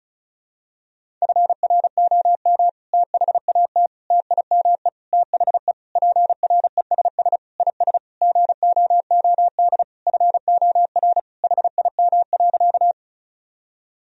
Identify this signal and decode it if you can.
Morse code, heard as a keyed tone, about 36 wpm, 700 Hz. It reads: FROM THAT TIME THE PRESS IS GOOD FOR HIM.
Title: Sentences from Top 1000 Words - Review Part 1 - 35wpm